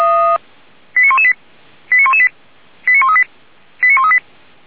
Five-tone selective call (ZVEI), heard as quick runs of five tones, tones of about 70 ms. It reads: 89118 89118 89138 89138